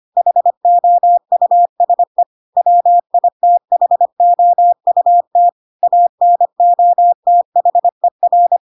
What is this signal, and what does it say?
Morse code, heard as a keyed tone, about 25 wpm, 700 Hz. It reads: HOUSE WITHOUT ANOTHER
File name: Pinocchio / Pinocchio ch8 25 25 ik1hge